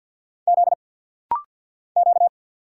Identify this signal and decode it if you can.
Morse code, heard as a keyed tone, about 50 wpm, 690 Hz. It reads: Z Q
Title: Single Letter-Number Alt - Rapid-Fire 50wpm